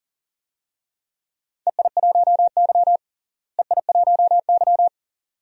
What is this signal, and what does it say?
Morse code, heard as a keyed tone, about 38 wpm, 700 Hz. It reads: EI1Y EI1Y
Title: DXCC-100 Call Signs - 4 Characters - Repeated 2x - 40wpm